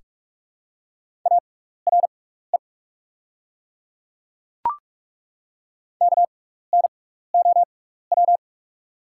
Morse code, heard as a keyed tone, about 48 wpm, spaced as for 16 wpm, 700 Hz. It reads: ARE KNOW
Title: 6x Character Spacing - Top 100 Words - 45wpm